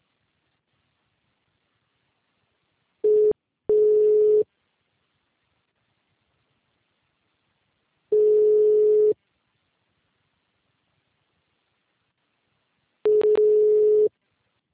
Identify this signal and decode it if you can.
Ringback tone (Europe, 425 Hz).
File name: out- -2010- - - .48449